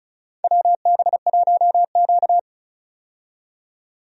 Morse code, heard as a keyed tone, about 36 wpm, 690 Hz. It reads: WB1Q
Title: USA Call Signs - 4 Characters - 35wpm